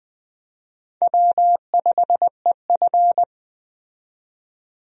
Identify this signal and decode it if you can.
Morse code, heard as a keyed tone, about 20 wpm, 700 Hz. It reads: W5EF
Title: USA Call Signs - 4 Characters - 20wpm